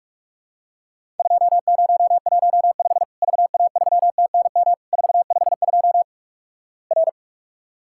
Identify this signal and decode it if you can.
Morse code, heard as a keyed tone, about 46 wpm, 700 Hz. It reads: 2015 VA3TNM 452 R
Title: Worked All Europe DX Contest - QTC Exchange - 45wpm